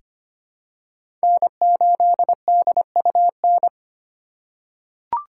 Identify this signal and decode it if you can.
Morse code, heard as a keyed tone, about 25 wpm, 700 Hz. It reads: N8DUN
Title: USA Call Signs - 5 Characters - 2x Character Spacing - 25wpm